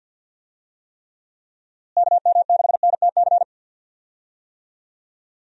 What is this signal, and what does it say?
Morse code, heard as a keyed tone, about 52 wpm, 700 Hz. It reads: KM6NTC
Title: USA Call Signs - 6 Characters - Repeated 3x - 50wpm